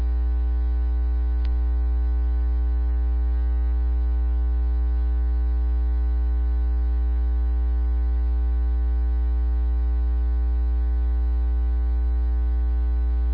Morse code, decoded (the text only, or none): none